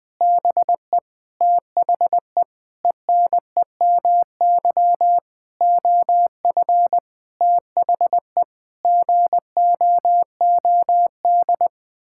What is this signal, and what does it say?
Morse code, heard as a keyed tone, about 20 wpm, 700 Hz. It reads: BE THE ENEMY OF THE GOOD
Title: Sets of 9 Words - 20wpm